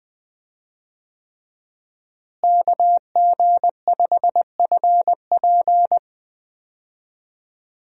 Morse code, heard as a keyed tone, about 21 wpm, 700 Hz. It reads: KG5FP
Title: USA Call Signs - 5 Characters - Repeated 3x - 20wpm